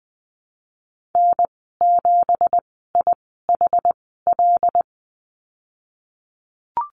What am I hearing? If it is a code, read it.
Morse code, heard as a keyed tone, about 20 wpm, spaced as for 14 wpm, 700 Hz. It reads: N7IHL
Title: 186 Lesson 30g - Letter b - Call Signs 20wpm